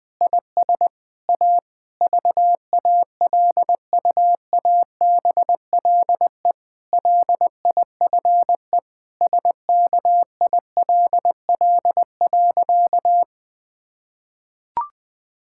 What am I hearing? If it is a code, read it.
Morse code, heard as a keyed tone, about 20 wpm, 700 Hz. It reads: IS A VALUABLE LIFE SKILL.
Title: Difficult Sentences with Hints - 20wpm